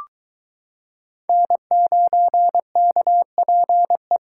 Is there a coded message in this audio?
Yes: Morse code from 1.3 to 4.2 s, heard as a keyed tone, about 22 wpm, 700 Hz. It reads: N9KPE